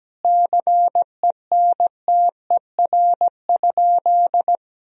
Morse code, heard as a keyed tone, about 17 wpm, 700 Hz. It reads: CENTER?
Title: Sentences from Top 1000 Words - Review Part 2 - 17wpm